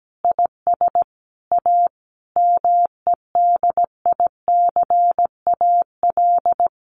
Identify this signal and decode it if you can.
Morse code, heard as a keyed tone, about 17 wpm, 700 Hz. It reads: IS A MEDICAL
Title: Additional Easy Phrases 17wpm